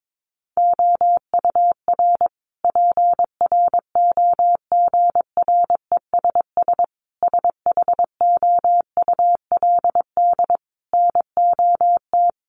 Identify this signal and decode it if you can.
Morse code, heard as a keyed tone, about 22 wpm, 700 Hz. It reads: OUR PROGRESS SHOULD NOT
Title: Sentences from Top 1400 Words 22wpm